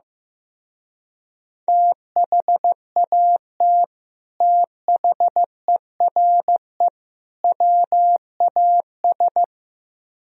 Morse code, heard as a keyed tone, about 15 wpm, 700 Hz. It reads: THAT THERE WAS
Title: Speed-Racing - Sets of 3 Words 15wpm